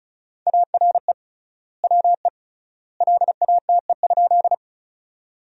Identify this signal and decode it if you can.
Morse code, heard as a keyed tone, about 36 wpm, 700 Hz. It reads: ARE WE LATE?
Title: 3x Word-Spacing - Sentences from Top 300 Words - 35wpm